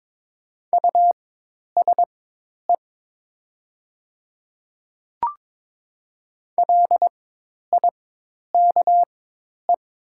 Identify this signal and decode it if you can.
Morse code, heard as a keyed tone, about 22 wpm, spaced as for 10 wpm, 700 Hz. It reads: USE LIKE
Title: 4x Character Spacing - Top 100 Words - 22wpm